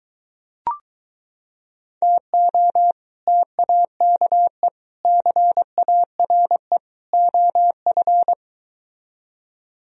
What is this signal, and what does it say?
Morse code, heard as a keyed tone, about 22 wpm, 700 Hz. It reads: TO TAKE CARE OF